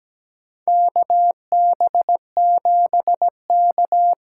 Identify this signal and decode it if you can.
Morse code, heard as a keyed tone, about 17 wpm, 700 Hz. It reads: KB7K
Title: USA Call Signs - 6 Characters - 8x Character Spacing - 17wpm